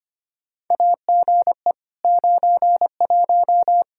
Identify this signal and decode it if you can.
Morse code, heard as a keyed tone, about 25 wpm, 700 Hz. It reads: AGE 91
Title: Practice QSO - Exchange 3 - More Info - 25wpm